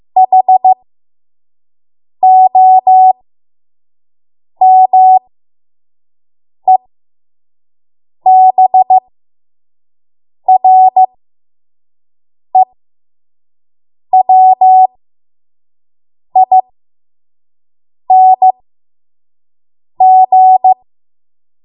Morse code, heard as a keyed tone, about 15 wpm, spaced as for 5 wpm, 750 Hz. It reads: HOMEBREWING
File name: Morse / 05WPM